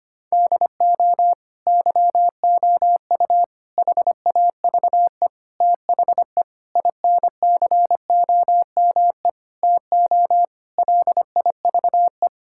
Morse code, heard as a keyed tone, about 25 wpm, 700 Hz. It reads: DO YOU HAVE THE INCOME TO LIVE
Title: Sentences from Top 1000 Words - Review Part 5 - 25wpm